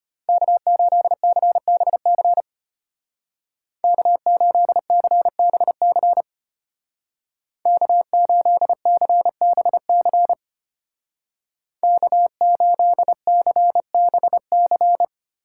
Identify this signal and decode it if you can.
Morse code, heard as a keyed tone, about 31 wpm, 700 Hz. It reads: K8CBC K8CBC K8CBC K8CBC